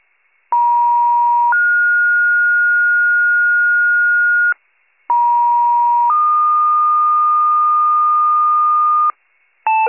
Two-tone page, heard as a one-second tone, then a three-second tone, about 950 then 1430 Hz, about 950 then 1220 Hz, with noise. It is heard from 0.5 to 9.1 s.